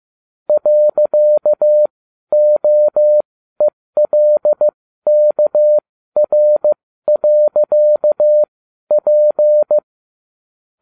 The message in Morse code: .OELKR.P

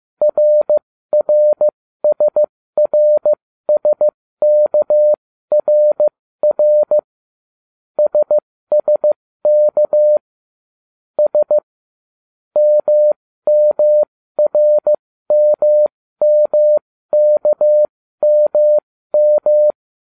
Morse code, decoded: RRSRSKRR SSK S MMRMMKMM

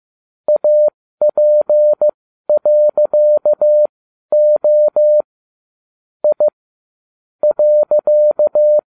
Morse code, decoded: AP.O I .

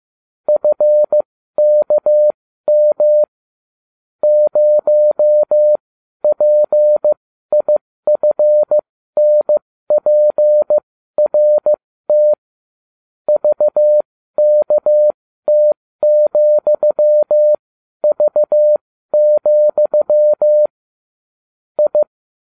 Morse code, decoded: FKM 0PIFNPRT VKT,V, I

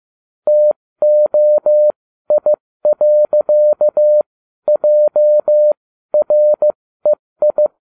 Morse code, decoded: TOI.JREI

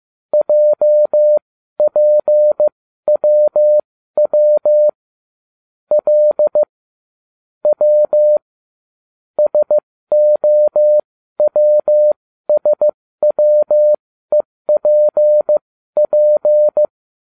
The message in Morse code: JPWW L W SOWSWEPP

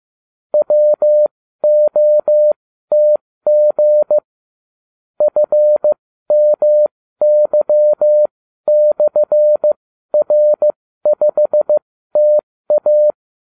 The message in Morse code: WOTG FMY/R5TA

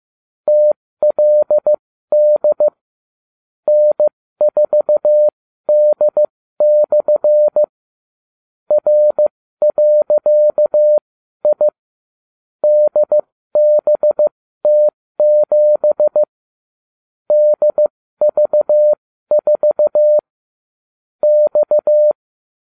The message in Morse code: TLD N4D/ R.I DBT7 DV4 X

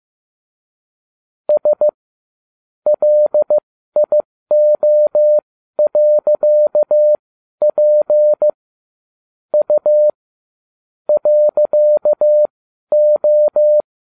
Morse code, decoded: S LIO.P U .O

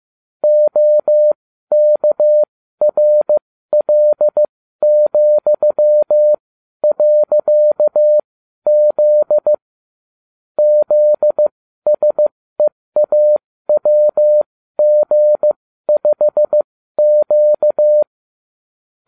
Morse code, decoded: OKRL,.Z ZSEAWG5Q

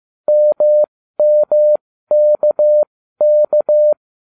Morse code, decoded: MMKK

